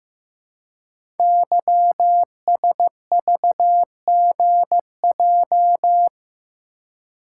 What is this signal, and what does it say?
Morse code, heard as a keyed tone, about 15 wpm, 700 Hz. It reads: YSVGJ
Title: Code Groups - 5 Letters - 15wpm